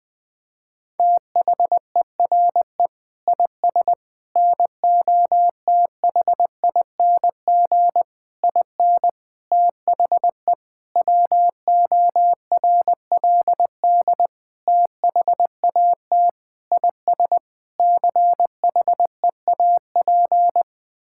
Morse code, heard as a keyed tone, about 21 wpm, 700 Hz. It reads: THERE IS NOTHING IN THE WORLD THAT IS CHEAP
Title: Sentences from Top 1600 Words 20wpm